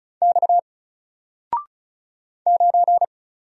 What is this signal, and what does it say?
Morse code, heard as a keyed tone, about 36 wpm, 700 Hz. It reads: X 9
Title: Single Letter-Number Alt 35wpm